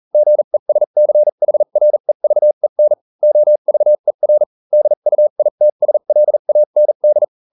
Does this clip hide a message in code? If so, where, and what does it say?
Yes: Morse code from 0.1 to 7.2 s, heard as a keyed tone, about 40 wpm, spaced as for 35 wpm, 600 Hz. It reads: GESCHREVEN OVER DUITSLAND